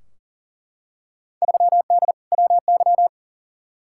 Morse code, heard as a keyed tone, about 40 wpm, 700 Hz. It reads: 3D WY
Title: Field Day Exchanges Encore - 40wpm